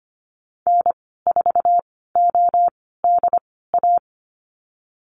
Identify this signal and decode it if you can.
Morse code, heard as a keyed tone, about 25 wpm, spaced as for 16 wpm, 700 Hz. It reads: N4ODA
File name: Lesson 15g - Number 5 - Call Signs 25wpm